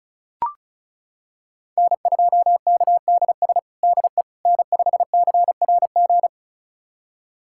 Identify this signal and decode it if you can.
Morse code, heard as a keyed tone, about 36 wpm, 700 Hz. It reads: N2KDS DE N5CRG
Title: USA Callsign DE Callsign - 5 Characters - 35wpm